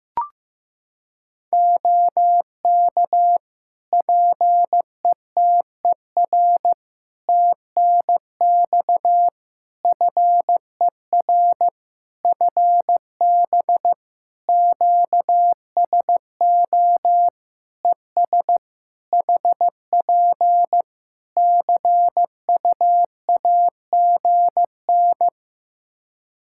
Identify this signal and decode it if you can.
Morse code, heard as a keyed tone, about 15 wpm, 700 Hz. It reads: OK PETER TNX FER FB QSO ES HP CU AGN
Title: Practice QSO - Exchange 4 - Ending - 15wpm